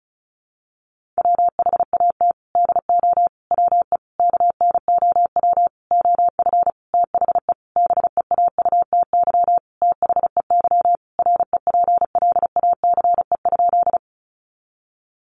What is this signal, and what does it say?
Morse code, heard as a keyed tone, about 34 wpm, 700 Hz. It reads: WHAT DO WE KNOW OF THE BEAUTY THEY REPLACE?
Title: Sentences from Top 1400 Words 35wpm